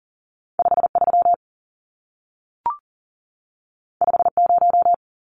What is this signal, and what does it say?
Morse code, heard as a keyed tone, about 40 wpm, 700 Hz. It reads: 53 50
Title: Numbers - 2-Digit 40wpm